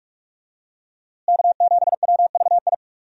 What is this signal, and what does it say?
Morse code, heard as a keyed tone, about 47 wpm, 700 Hz. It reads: K7WVI
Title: USA Call Signs - 5 Characters - Repeated 2x - 45wpm